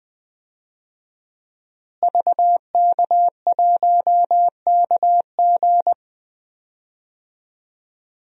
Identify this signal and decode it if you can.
Morse code, heard as a keyed tone, about 20 wpm, 700 Hz. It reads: VK1KG
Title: DXCC-100 Call Signs - 5 Characters - Repeated 2x - 20wpm